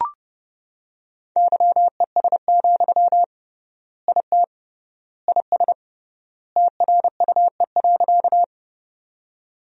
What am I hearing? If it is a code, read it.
Morse code, heard as a keyed tone, about 30 wpm, 700 Hz. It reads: YES, IT IS TRUE.
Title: 3x Word-Spacing - Sentences from Top 500 Words - 30wpm